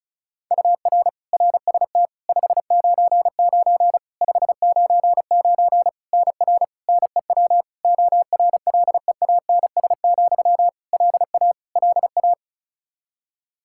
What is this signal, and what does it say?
Morse code, heard as a keyed tone, about 36 wpm, 700 Hz. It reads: UR RST 599 599 NR NEW ORLEANS, LA LA
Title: Practice QSO - Exchange 1 - Beginning - 35wpm